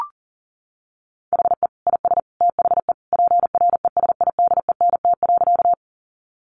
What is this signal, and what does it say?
Morse code, heard as a keyed tone, about 40 wpm, 700 Hz. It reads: HE IS THE PRESIDENT.
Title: Sentences from Top 1000 Words 40wpm